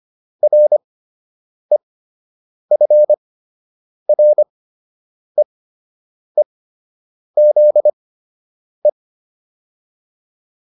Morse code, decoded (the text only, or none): REFREEZE